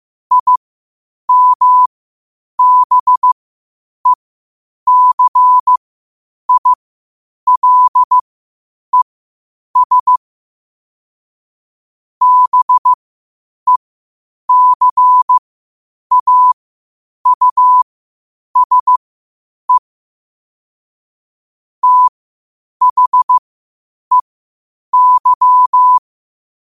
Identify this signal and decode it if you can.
Morse code, heard as a keyed tone, about 15 wpm, spaced as for 8 wpm, 1000 Hz. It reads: IMBECILES BECAUSE THEY
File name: qotdfiles / 8wpm